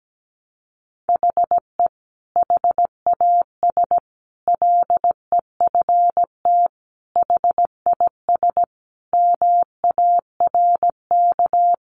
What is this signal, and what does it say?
Morse code, heard as a keyed tone, about 17 wpm, 700 Hz. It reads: HE HAS LEFT HIS MARK
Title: Sentences from Top 1100 Words 17wpm